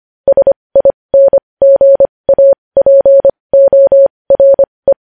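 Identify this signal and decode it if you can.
Morse code, heard as a keyed tone, about 25 wpm, spaced as for 20 wpm, 550 Hz. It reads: SINGAPORE
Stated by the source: Morse code news headlines sent at 20.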